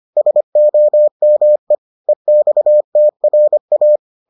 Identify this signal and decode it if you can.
Morse code, heard as a keyed tone, about 25 wpm, 600 Hz. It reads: SOME EXTRA